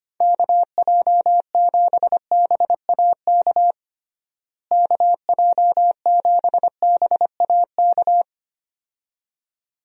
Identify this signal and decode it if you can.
Morse code, heard as a keyed tone, about 25 wpm, 700 Hz. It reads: KJ7BAK KJ7BAK
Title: USA Call Signs - 6 Characters - Repeated 2x - 25wpm